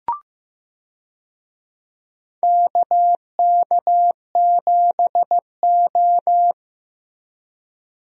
Morse code, heard as a keyed tone, about 15 wpm, 700 Hz. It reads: KK7O
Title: USA Call Signs - 4 Characters - Repeated 3x - 15wpm